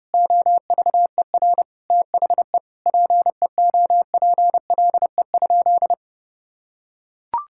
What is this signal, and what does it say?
Morse code, heard as a keyed tone, about 30 wpm, 700 Hz. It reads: OVER THE PEOPLE?
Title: Sentences from Top 1000 Words - Review Part 5 - 30wpm